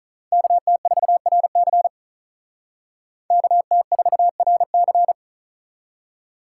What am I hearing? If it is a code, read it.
Morse code, heard as a keyed tone, about 46 wpm, 700 Hz. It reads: KT4RC KT4RC